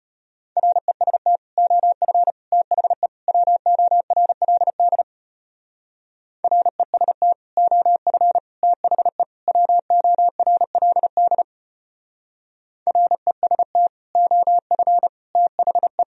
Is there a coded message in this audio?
Yes: Morse code from 0.6 to 16.0 s, heard as a keyed tone, about 32 wpm, 700 Hz. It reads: REST OF THE WORLD REST OF THE WORLD REST OF THE